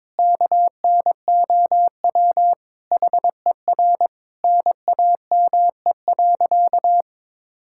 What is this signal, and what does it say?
Morse code, heard as a keyed tone, about 22 wpm, 700 Hz. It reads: KNOW HER NAME.
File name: practice / 22wpm / Sentences with 4-Word Phrases - 22wpm